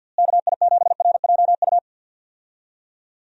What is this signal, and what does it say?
Morse code, heard as a keyed tone, about 48 wpm, 700 Hz. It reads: KI7RJU
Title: USA Call Signs - 6 Characters - Repeated 3x - 50wpm